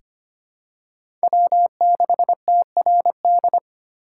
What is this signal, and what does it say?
Morse code, heard as a keyed tone, about 25 wpm, 700 Hz. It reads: W6TRD
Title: USA Call Signs - 3x Character Spacing - 25wpm